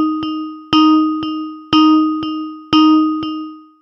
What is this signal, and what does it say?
There is no code.